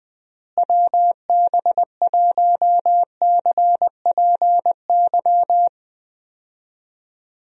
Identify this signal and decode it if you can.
Morse code, heard as a keyed tone, about 20 wpm, 700 Hz. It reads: WB1CPY
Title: USA Call Signs - 6 Characters - Repeated 2x - 20wpm